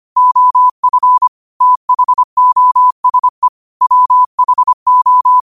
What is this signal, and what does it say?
Morse code, heard as a keyed tone, about 25 wpm, 1000 Hz. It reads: OF THOSE WHO
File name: qotdfiles / 25wpm